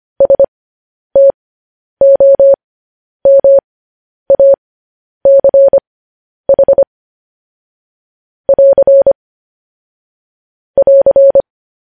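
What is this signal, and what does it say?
Morse code, heard as a keyed tone, about 25 wpm, spaced as for 10 wpm, 550 Hz. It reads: STOMACH <AR> <AR>
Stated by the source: Morse code news headlines sent at 10 wpm for practice.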